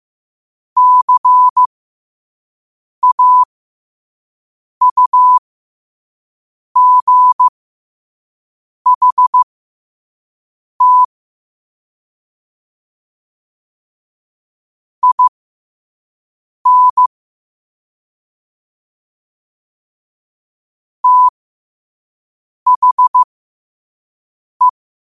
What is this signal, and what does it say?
Morse code, heard as a keyed tone, about 15 wpm, spaced as for 5 wpm, 1000 Hz. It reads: CAUGHT IN THE
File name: qotdfiles / 5wpm